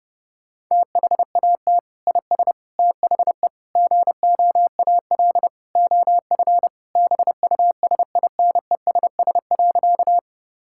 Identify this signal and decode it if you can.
Morse code, heard as a keyed tone, about 31 wpm, 700 Hz. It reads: THAT IS THE GOAL OF BUSINESS.
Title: Sentences from Top 1000 Words - Review Part 5 - 30wpm